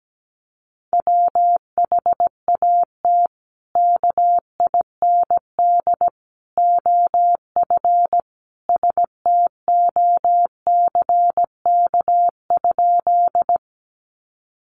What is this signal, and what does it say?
Morse code, heard as a keyed tone, about 17 wpm, 700 Hz. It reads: WHAT KIND OF STOCK?
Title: Sentences from Top 800 Words 17wpm